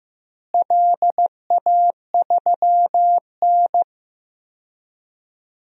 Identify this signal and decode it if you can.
Morse code, heard as a keyed tone, about 15 wpm, 700 Hz. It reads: LA3N
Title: DXCC-100 Call Signs - 4 Characters - Repeated 2x - 15wpm